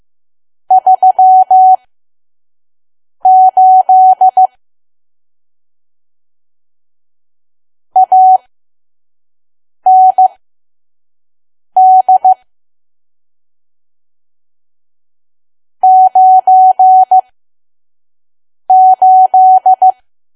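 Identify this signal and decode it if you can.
Morse code, heard as a keyed tone, about 15 wpm, spaced as for 5 wpm, 750 Hz. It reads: 38 AND 98
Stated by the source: Morse code practice at 5 WPM from KQ2P.COM.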